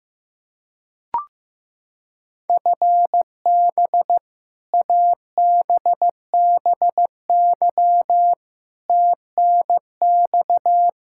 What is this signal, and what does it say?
Morse code, heard as a keyed tone, about 15 wpm, 700 Hz. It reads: FB ABBY TNX
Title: Practice QSO - Exchange 3 - More Info - 15wpm